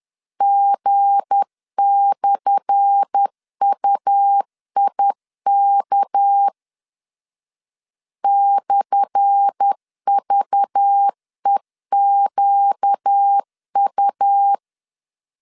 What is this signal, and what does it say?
Morse code, heard as a keyed tone, about 10 wpm, 800 Hz. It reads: G/UIK /VEQU
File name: LEZIONI / 10 SIGNS / 10A SLASH 10wpm 2808